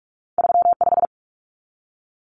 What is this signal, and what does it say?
Morse code, heard as a keyed tone, about 46 wpm, 700 Hz. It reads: 35
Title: Common Cut Numbers - 2-Digit 45wpm